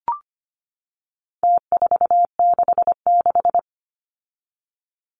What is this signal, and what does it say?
Morse code, heard as a keyed tone, about 25 wpm, 700 Hz. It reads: T466
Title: Numbers - Contest Style - 4-Digit 25wpm